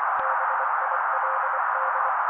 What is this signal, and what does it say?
Morse code, heard as a keyed tone, about 24 wpm, 530 Hz. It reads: BILD
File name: daily / 23wpm noise